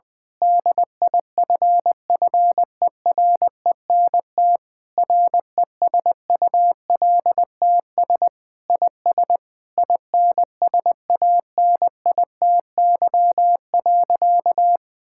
Morse code, heard as a keyed tone, about 20 wpm, 700 Hz. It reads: DIFFERENT RESULTS IS INSANITY.
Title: Sentences with 9-Word Phrases - Spoken Beginning - 20wpm